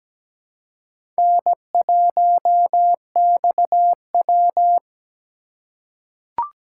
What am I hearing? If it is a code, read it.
Morse code, heard as a keyed tone, about 17 wpm, 700 Hz. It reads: N1XW
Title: USA Call Signs - 4 Characters - 4x Character Spacing - 17wpm